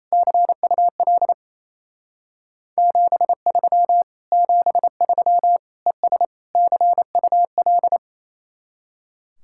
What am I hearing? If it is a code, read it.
Morse code, heard as a keyed tone, about 34 wpm, 700 Hz. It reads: CUL 73 73 ES CUL
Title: Speed-Racing Common QSO Phrases - 28wpm